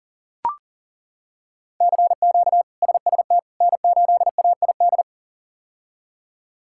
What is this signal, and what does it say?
Morse code, heard as a keyed tone, about 40 wpm, 700 Hz. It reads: CQ SST N8AID
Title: K1USN SST QSO Practice - 40wpm